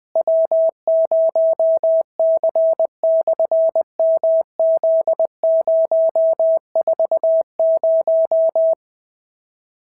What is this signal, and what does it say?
Morse code, heard as a keyed tone, about 20 wpm, 650 Hz. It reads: W0C/MZ040
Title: SOTA QSO Practice 20wpm